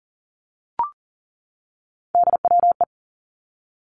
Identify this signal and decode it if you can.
Morse code, heard as a keyed tone, about 43 wpm, 700 Hz. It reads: DWE
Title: Three Challenging Letter-Number Combinations 40wpm